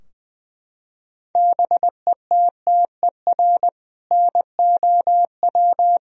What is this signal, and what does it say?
Morse code, heard as a keyed tone, about 21 wpm, 700 Hz. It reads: BETTER NOW
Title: Sentences from Top 1000 Words - Review Part 4 - 20wpm